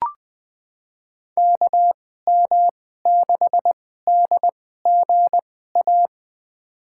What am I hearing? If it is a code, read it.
Morse code, heard as a keyed tone, about 20 wpm, spaced as for 15 wpm, 700 Hz. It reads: KM6DGA
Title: USA Call Signs - 6 Characters - 2x Character Spacing - 20wpm